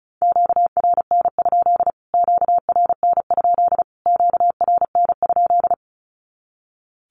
Morse code, heard as a keyed tone, about 34 wpm, 700 Hz. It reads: QRN? QRN? QRN?
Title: Q-Codes Repeated 3x 35wpm